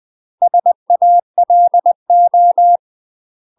Morse code, heard as a keyed tone, about 20 wpm, 700 Hz. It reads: SALO
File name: UK MASTER QSOs TRAINING / 20WPM / UK MASTER QSOs 0024